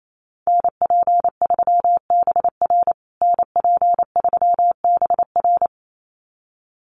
Morse code, heard as a keyed tone, about 28 wpm, 700 Hz. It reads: NP3BR NP3BR
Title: USA Call Signs Repeated 2x 28wpm